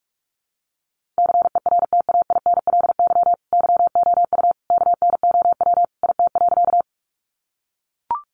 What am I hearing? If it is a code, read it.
Morse code, heard as a keyed tone, about 44 wpm, 700 Hz. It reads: CERTAINLY YOU KNOW IT.
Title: Sentences from Top 900 Words 45wpm